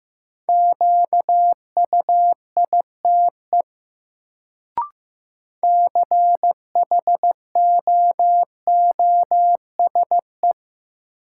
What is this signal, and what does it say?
Morse code, heard as a keyed tone, about 15 wpm, 700 Hz. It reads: QUITE CHOOSE
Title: Sentences from Top 700 Words - New Word Drill - 15wpm